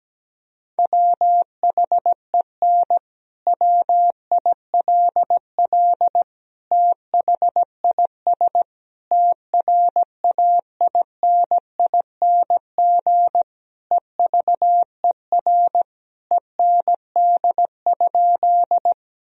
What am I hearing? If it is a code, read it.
Morse code, heard as a keyed tone, about 17 wpm, 700 Hz. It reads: WHEN WILL THIS TRAINING EVER END?
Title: Sentences from Top 1000 Words - Review Part 4 - 17wpm